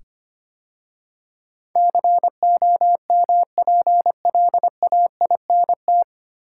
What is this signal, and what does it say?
Morse code, heard as a keyed tone, about 25 wpm, 700 Hz. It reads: COMPLAINT
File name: practice / 25wpm / 9 Letter Words - First 3 Letters Given - 25wpm